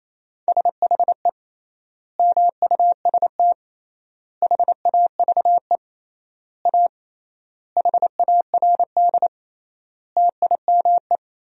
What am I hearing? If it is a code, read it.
Morse code, heard as a keyed tone, about 28 wpm, 700 Hz. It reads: SHE MUST HAVE A HARD TIME
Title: 3x Word-Spacing - Sentences from Top 300 Words - 28wpm